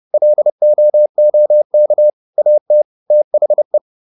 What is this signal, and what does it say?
Morse code, heard as a keyed tone, about 30 wpm, 600 Hz. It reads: LOOK AT THE
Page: advanced-practice-files